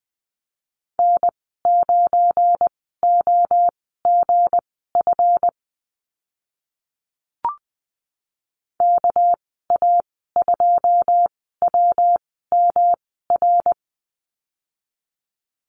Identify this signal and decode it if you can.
Morse code, heard as a keyed tone, about 20 wpm, spaced as for 14 wpm, 700 Hz. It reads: N9OGF KA2WMR
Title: 245 Lesson 39g - Letter z - Call Signs 20wpm